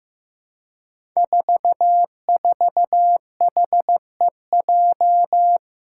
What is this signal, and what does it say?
Morse code, heard as a keyed tone, about 15 wpm, 700 Hz. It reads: 44HEJ
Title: Five Challenging Letter-Number Combinations 15wpm